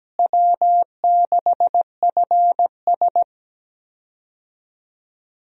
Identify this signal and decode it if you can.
Morse code, heard as a keyed tone, about 17 wpm, 700 Hz. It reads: W6FS
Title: USA Call Signs - 4 Characters - Repeated 2x - 17wpm